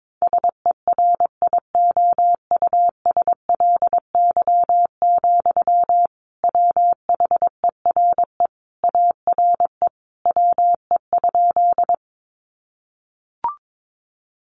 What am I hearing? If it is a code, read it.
Morse code, heard as a keyed tone, about 22 wpm, 700 Hz. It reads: SERIOUSLY, WHERE ARE WE?
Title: Sentences from Top 1200 Words 22wpm